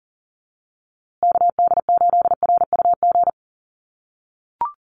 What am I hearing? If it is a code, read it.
Morse code, heard as a keyed tone, about 40 wpm, 700 Hz. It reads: KD8RUG
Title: USA Call Signs Repeated 2x 40wpm